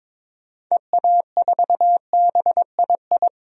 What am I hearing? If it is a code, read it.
Morse code, heard as a keyed tone, about 22 wpm, 700 Hz. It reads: EA4BII
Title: DXCC-100 Call Signs - 6 Characters - 22wpm